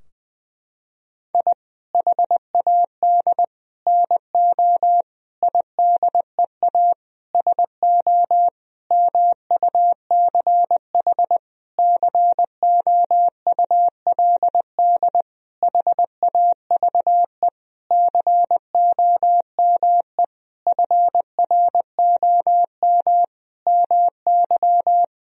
Morse code, heard as a keyed tone, about 21 wpm, 700 Hz. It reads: I HAD NO IDEA SO MUCH COULD HAVE COME FROM MY
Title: Sentences from Top 1000 Words - Review Part 4 - 20wpm